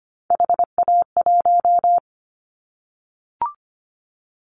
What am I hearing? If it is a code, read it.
Morse code, heard as a keyed tone, about 25 wpm, 700 Hz. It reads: HA1
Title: Callsign Prefixes for Top 25 DXCC - 25wpm